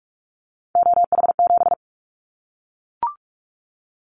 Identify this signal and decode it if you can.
Morse code, heard as a keyed tone, about 44 wpm, 700 Hz. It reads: OH7
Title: Callsign Prefixes for Top 25 DXCC - Repeated 3x - 45wpm